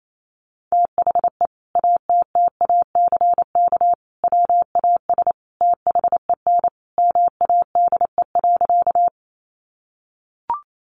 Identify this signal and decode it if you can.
Morse code, heard as a keyed tone, about 28 wpm, 700 Hz. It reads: THE ATTACK WAS THEN MADE.
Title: Sentences from Top 1500 Words 28wpm